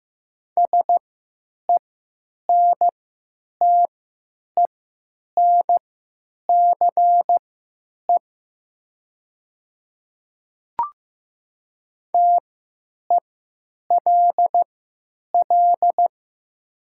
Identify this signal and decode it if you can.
Morse code, heard as a keyed tone, about 15 wpm, spaced as for 9 wpm, 700 Hz. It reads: SENTENCE TELL
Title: 3x Character Spacing - Top 500 Words - 15wpm